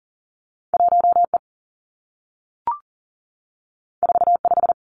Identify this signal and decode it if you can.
Morse code, heard as a keyed tone, about 39 wpm, 700 Hz. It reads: 1E 45
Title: Two Challenging Letter-Number Combinations 40wpm